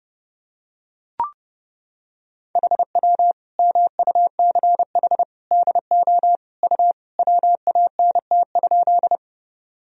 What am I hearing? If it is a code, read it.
Morse code, heard as a keyed tone, about 32 wpm, 700 Hz. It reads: HW MUCH DO U WANT?
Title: Sentences from Top 200 Words and Abbreviations 30wpm